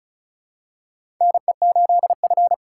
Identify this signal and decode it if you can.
Morse code, heard as a keyed tone, about 36 wpm, 700 Hz. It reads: NE8F
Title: USA Call Signs - 4 Characters - 3x Character Spacing - 35wpm